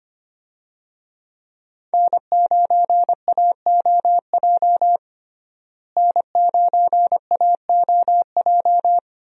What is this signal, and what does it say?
Morse code, heard as a keyed tone, about 25 wpm, 700 Hz. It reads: N9AOJ N9AOJ
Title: USA Call Signs - 5 Characters - Repeated 3x - 25wpm